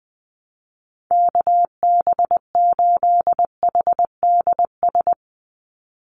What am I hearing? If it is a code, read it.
Morse code, heard as a keyed tone, about 20 wpm, 700 Hz. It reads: KB8HDS
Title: USA Call Signs 20wpm